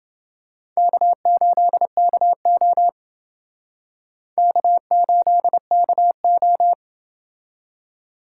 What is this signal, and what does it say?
Morse code, heard as a keyed tone, about 27 wpm, 700 Hz. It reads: K8KO K8KO